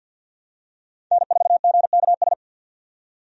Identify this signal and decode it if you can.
Morse code, heard as a keyed tone, about 45 wpm, 700 Hz. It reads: N4GKS
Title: USA Call Signs - 5 Characters - 50wpm